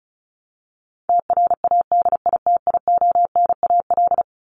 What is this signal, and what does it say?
Morse code, heard as a keyed tone, about 34 wpm, 700 Hz. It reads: TRADITIONAL
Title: Top 751-1000 Words 35wpm